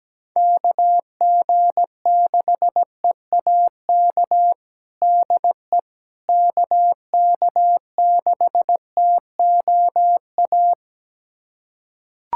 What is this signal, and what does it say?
Morse code, heard as a keyed tone, about 17 wpm, 700 Hz. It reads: KG6EAK DE KK6TOA
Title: USA Callsign DE Callsign - 6 Characters - 17wpm